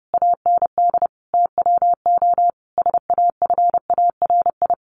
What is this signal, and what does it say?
Morse code, heard as a keyed tone, about 30 wpm, 700 Hz. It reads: AND TWO SAFARI
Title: A Sound Of Thunder 30wpm